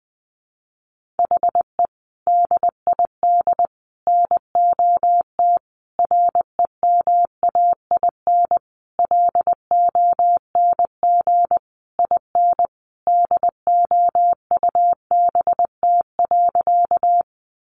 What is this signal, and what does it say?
Morse code, heard as a keyed tone, about 20 wpm, 700 Hz. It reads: HE DID NOT REMAIN LONG IN DOUBT.
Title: Sentences from Top 1500 Words 20wpm